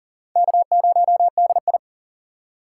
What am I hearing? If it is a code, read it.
Morse code, heard as a keyed tone, about 39 wpm, 700 Hz. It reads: K0DI
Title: USA Call Signs - 4 Characters - 2x Character Spacing - 40wpm